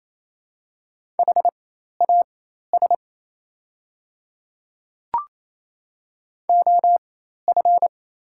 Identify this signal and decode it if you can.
Morse code, heard as a keyed tone, about 28 wpm, spaced as for 13 wpm, 700 Hz. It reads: HAS OF
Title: 4x Character Spacing - Top 300 Words - 28wpm